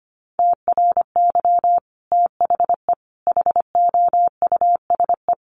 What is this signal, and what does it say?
Morse code, heard as a keyed tone, about 25 wpm, 700 Hz. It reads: TRY THE HOUSE
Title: Sentences from Top 1300 Words 25wpm